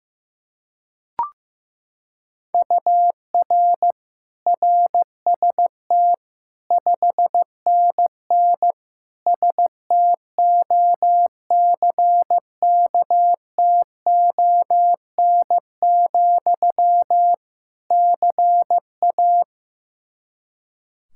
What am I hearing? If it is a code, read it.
Morse code, heard as a keyed tone, about 15 wpm, 700 Hz. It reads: UR RST 5NN STOCKTON, CA
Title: K1USN SST QSO Practice 15wpm